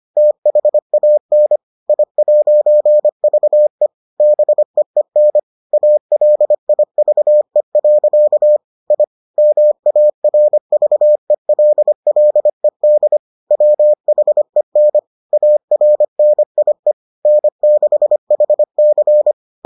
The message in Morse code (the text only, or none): THAN I'VE BEEN ALIVE. I MARVELLED WHEN ARNIE N6HC